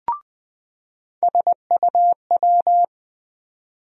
Morse code, heard as a keyed tone, about 20 wpm, 700 Hz. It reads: SUW
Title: USA Call Sign Postfixes - 3 Letters 20wpm